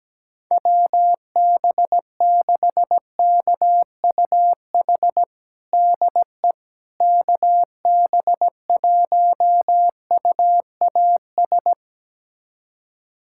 Morse code, heard as a keyed tone, about 17 wpm, 700 Hz. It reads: WB6KUH DE KB1UAS
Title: USA Callsign DE Callsign - 6 Characters - 17wpm